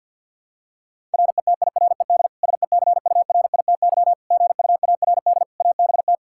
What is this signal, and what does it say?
Morse code, heard as a keyed tone, about 52 wpm, 700 Hz. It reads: RETIRED SECURITY GUARD ABT
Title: Practice QSO - Exchange 3 - More Info - 50wpm